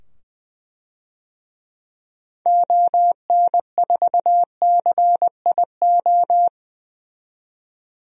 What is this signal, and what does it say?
Morse code, heard as a keyed tone, about 20 wpm, 700 Hz. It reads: ON4CIO
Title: DXCC-100 Call Signs - 6 Characters - 20wpm